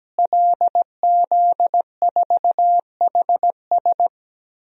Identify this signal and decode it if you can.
Morse code, heard as a keyed tone, about 17 wpm, 700 Hz. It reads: LZ4HS
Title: DXCC-100 Call Signs - 5 Characters - Repeated 3x - 17wpm